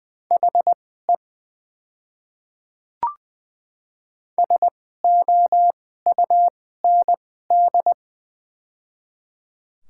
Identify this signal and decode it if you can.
Morse code, heard as a keyed tone, about 21 wpm, spaced as for 15 wpm, 700 Hz. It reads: HE SOUND
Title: 2x Character Spacing - Top 100 Words - 20wpm